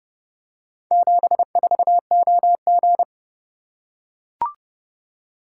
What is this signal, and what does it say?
Morse code, heard as a keyed tone, about 29 wpm, 700 Hz. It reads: 74OG